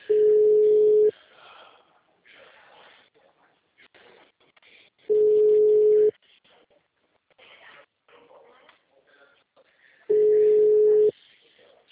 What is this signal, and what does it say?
Ringback tone (Europe, 425 Hz).